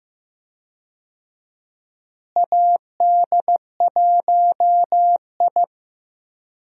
Morse code, heard as a keyed tone, about 15 wpm, 700 Hz. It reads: AD1I
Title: USA Call Signs - 4 Characters - Repeated 2x - 15wpm